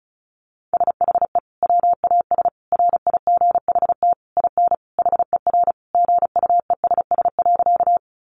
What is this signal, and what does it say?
Morse code, heard as a keyed tone, about 34 wpm, 700 Hz. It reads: SHE WAS RIGHT IN HER GUESS.
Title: Sentences from Top 1500 Words 35wpm